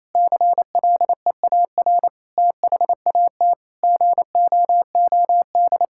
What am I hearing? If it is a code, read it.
Morse code, heard as a keyed tone, about 28 wpm, 700 Hz. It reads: CLEAR THAT GOOD
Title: Sentences from Top 1000 Words - Review Part 6 - 28wpm